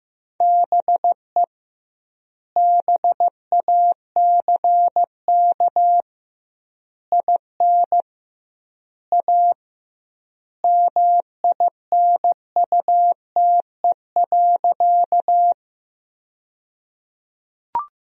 Morse code, heard as a keyed tone, about 15 wpm, 700 Hz. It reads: BE BACK IN A MINUTE.
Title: 2x Word-Spacing - Sentences from Top 500 Words - 15wpm